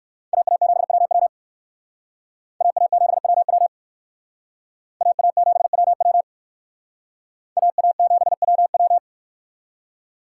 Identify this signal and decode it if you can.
Morse code, heard as a keyed tone, about 56 wpm, 700 Hz. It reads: AA7WW AA7WW AA7WW AA7WW